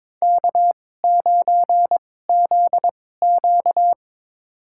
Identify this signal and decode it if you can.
Morse code, heard as a keyed tone, about 23 wpm, spaced as for 16 wpm, 700 Hz. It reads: K9ZQ
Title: USA Call Signs - 4 Characters - 2x Character Spacing - 22wpm